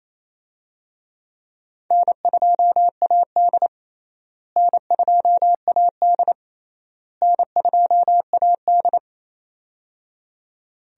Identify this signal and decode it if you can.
Morse code, heard as a keyed tone, about 28 wpm, 700 Hz. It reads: N2AD N2AD N2AD
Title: USA Call Signs - 4 Characters - Repeated 3x - 28wpm